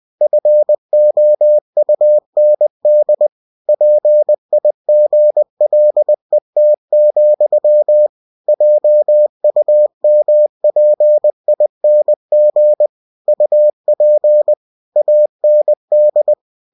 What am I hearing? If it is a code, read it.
Morse code, heard as a keyed tone, about 20 wpm, 600 Hz. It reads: FOUND PIGLET, JUMPING UP AND
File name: cw-books / winnie-the-pooh / winnie-20wpm-page-0048